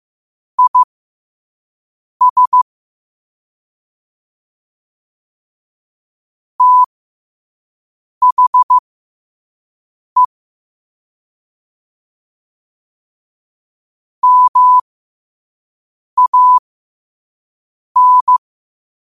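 Morse code, heard as a keyed tone, about 15 wpm, spaced as for 5 wpm, 1000 Hz. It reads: IS THE MAN
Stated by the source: Quote-of-the-day Morse code practice at 5 wpm.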